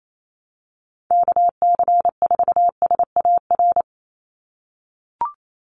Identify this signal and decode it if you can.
Morse code, heard as a keyed tone, about 28 wpm, 700 Hz. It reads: KC4SAR
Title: USA Call Signs 28wpm